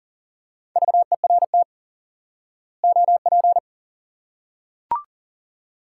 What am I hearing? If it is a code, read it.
Morse code, heard as a keyed tone, about 40 wpm, 700 Hz. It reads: VERT OP
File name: IQE Course 3 - Lesson 89 - YAGI - Two Elements - 40wpm